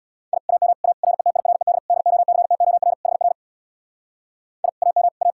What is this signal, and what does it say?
Morse code, heard as a keyed tone, about 66 wpm, 700 Hz. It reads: I AM A RETIRED DOCTOR HR I AM A